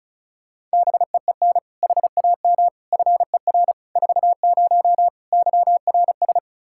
Morse code, heard as a keyed tone, about 36 wpm, 700 Hz. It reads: BEEN HAM FER 40 YRS
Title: Practice QSO - Exchange 3 - More Info - 35wpm